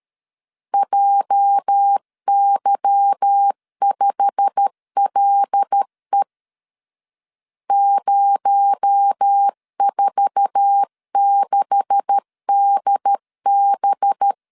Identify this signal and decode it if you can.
Morse code, heard as a keyed tone, about 13 wpm, 800 Hz. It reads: JY5LE 046DB
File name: LEZIONI / 9 286 12wpm 2305